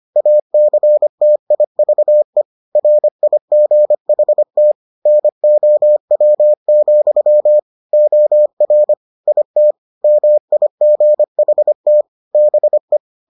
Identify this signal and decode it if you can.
Morse code, heard as a keyed tone, about 25 wpm, 600 Hz. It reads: ACTIVE RIGHT NOW, OR IT MIGHT BE